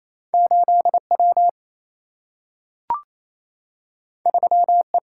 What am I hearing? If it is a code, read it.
Morse code, heard as a keyed tone, about 28 wpm, 700 Hz. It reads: 8 W 3 E
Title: Single-Letter Single-Number 28wpm